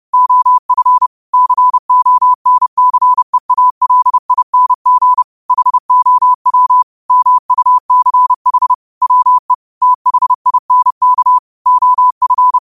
Morse code, heard as a keyed tone, about 30 wpm, 1000 Hz. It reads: OF CONCEALING HOW MUCH WE THINK OF